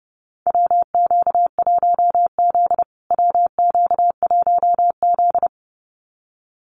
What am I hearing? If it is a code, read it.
Morse code, heard as a keyed tone, about 30 wpm, 700 Hz. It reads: WQ1Z WQ1Z
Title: USA Call Signs Repeated 2x 30wpm